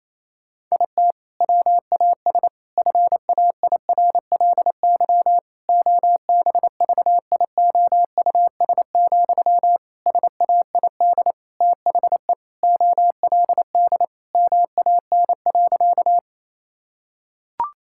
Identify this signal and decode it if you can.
Morse code, heard as a keyed tone, about 28 wpm, 700 Hz. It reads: IT WAS FAIRLY OBVIOUS, SAID THE OLD MAN.
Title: Sentences from Top 1600 Words 28wpm